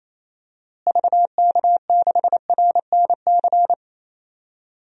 Morse code, heard as a keyed tone, about 28 wpm, 700 Hz. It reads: VK6RNC
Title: DXCC-100 Call Signs - 6 Characters - Repeated 3x - 28wpm